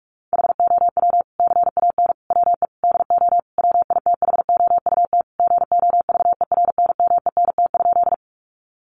Morse code, heard as a keyed tone, about 44 wpm, 700 Hz. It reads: HOW CAN WE DO WITHOUT GOVERNMENT?
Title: Sentences from Top 600 Words 45wpm